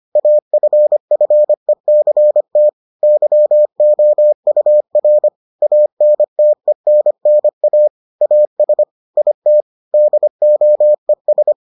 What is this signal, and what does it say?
Morse code, heard as a keyed tone, about 25 wpm, 600 Hz. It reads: AFFECT YOUR ANTENNA AS IT DOES